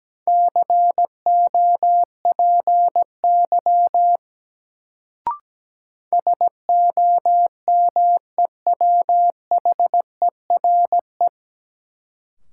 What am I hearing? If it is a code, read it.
Morse code, heard as a keyed tone, about 17 wpm, 700 Hz. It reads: COPY SOMEWHERE
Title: Sentences from Top 1400 Words - New Word Drill - 17wpm